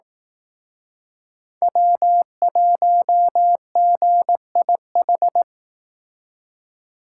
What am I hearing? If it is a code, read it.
Morse code, heard as a keyed tone, about 18 wpm, 700 Hz. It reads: W1GIH
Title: Speed Racing - USA 1x3 Call Signs 15wpm